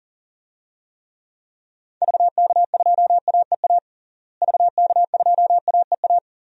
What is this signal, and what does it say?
Morse code, heard as a keyed tone, about 40 wpm, 700 Hz. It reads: VK2AEA VK2AEA
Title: DXCC-100 Call Signs - 6 Characters - Repeated 2x - 40wpm